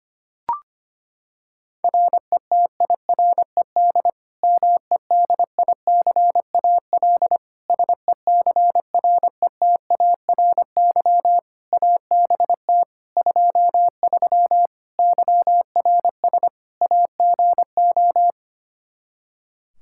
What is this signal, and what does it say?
Morse code, heard as a keyed tone, about 25 wpm, 700 Hz. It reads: RETIRED MEDICAL SECRETARY ABT 23 YRS AGO
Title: K1USN SST QSO Practice 25wpm